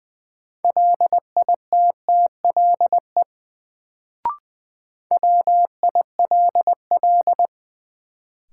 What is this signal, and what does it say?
Morse code, heard as a keyed tone, about 21 wpm, 700 Hz. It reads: LITTLE WILL
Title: Sets of 2 Words - New Word Drill - 20wpm